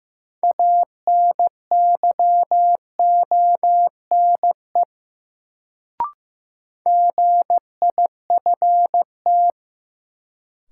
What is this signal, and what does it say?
Morse code, heard as a keyed tone, about 15 wpm, 700 Hz. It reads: ANYONE GIFT
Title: Sentences from Top 900 Words - New Word Drill - 15wpm